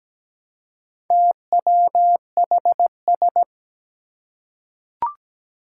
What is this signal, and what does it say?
Morse code, heard as a keyed tone, about 17 wpm, 700 Hz. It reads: TWHS
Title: Four Challenging Letter-Number Combinations 17wpm